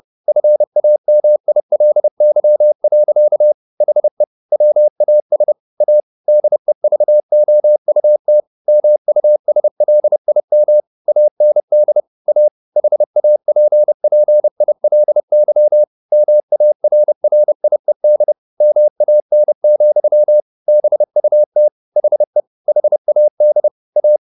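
Morse code, heard as a keyed tone, about 30 wpm, 600 Hz. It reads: FAMILY. HE WAS A DEVOUT MUSLIM AND A HAPPILY MARRIED MAN, BUT HE HAD A
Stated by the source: Morse code at 30 wpm, the novel Never by Ken Follett in English, from CW med Gnister.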